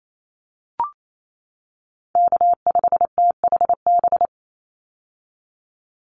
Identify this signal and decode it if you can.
Morse code, heard as a keyed tone, about 28 wpm, 700 Hz. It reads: K5THB
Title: USA Call Signs 28wpm